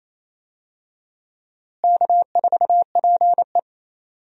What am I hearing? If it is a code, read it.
Morse code, heard as a keyed tone, about 28 wpm, 700 Hz. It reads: K4PE
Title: USA Call Signs - 4 Characters - Repeated 3x - 28wpm